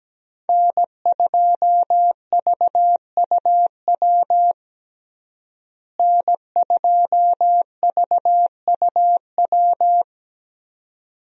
Morse code, heard as a keyed tone, about 17 wpm, 700 Hz. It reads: N2VUW N2VUW
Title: USA Call Signs - 5 Characters - Repeated 3x - 17wpm